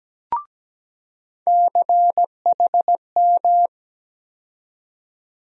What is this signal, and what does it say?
Morse code, heard as a keyed tone, about 17 wpm, 700 Hz. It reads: CHM
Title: USA Call Sign Postfixes - 3 Letters 17wpm